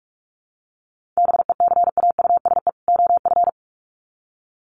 Morse code, heard as a keyed tone, about 44 wpm, 700 Hz. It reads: BECAUSE OF
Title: Sets of 2 Words 45wpm